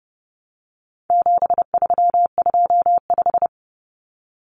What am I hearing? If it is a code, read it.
Morse code, heard as a keyed tone, about 30 wpm, 700 Hz. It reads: 7325
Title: Numbers - 4-Digit 30wpm